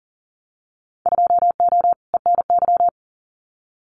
Morse code, heard as a keyed tone, about 40 wpm, 700 Hz. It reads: 2O ENY
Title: Winter Field Day Exchanges 40wpm